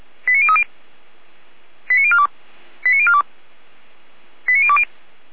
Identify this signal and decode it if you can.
Five-tone selective call (ZVEI), heard as quick runs of five tones, tones of about 70 ms. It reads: 89030 89042 89042 89020